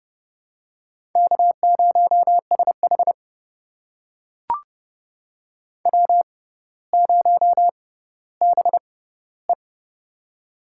Morse code, heard as a keyed tone, about 30 wpm, spaced as for 15 wpm, 700 Hz. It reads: K0SH W0BE
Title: USA Call Signs - 4 Characters - 6x Character Spacing - 30wpm